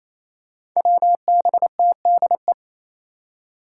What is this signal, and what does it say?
Morse code, heard as a keyed tone, about 28 wpm, 700 Hz. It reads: WBTDE
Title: Five Challenging Letter-Number Combinations 28wpm